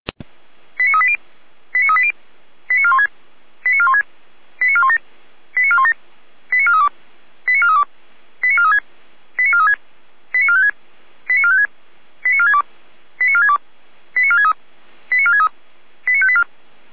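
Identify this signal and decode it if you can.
Five-tone selective call (ZVEI), heard as quick runs of five tones, tones of about 70 ms. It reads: 89380 89380 89416 89416 89417 89417 89432 89432 89436 89436 89456 89456 89462 89462 89463 89463 89584
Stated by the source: ZVEI five-tone selective call 89402 received by SDR.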